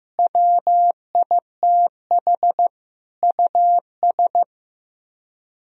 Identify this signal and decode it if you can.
Morse code, heard as a keyed tone, about 15 wpm, 700 Hz. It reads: WITH US
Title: Speed-Racing - Sets of 5 Words 15wpm